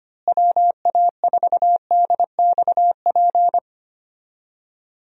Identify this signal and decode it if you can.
Morse code, heard as a keyed tone, about 25 wpm, 700 Hz. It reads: WA4DXP
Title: USA Call Signs - 6 Characters - Repeated 3x - 25wpm